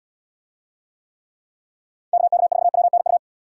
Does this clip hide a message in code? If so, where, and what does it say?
Yes: Morse code from 2.1 to 3.2 s, heard as a keyed tone, about 77 wpm, 700 Hz. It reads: KK4WNU